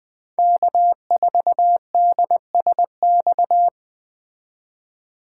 Morse code, heard as a keyed tone, about 20 wpm, 700 Hz. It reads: K4DSX
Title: 2023 POTA Activator Call Signs - 20wpm